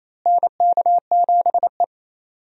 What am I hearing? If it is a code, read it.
Morse code, heard as a keyed tone, about 28 wpm, 700 Hz. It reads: NK7E
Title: USA Call Signs - 4 Characters - Repeated 2x - 28wpm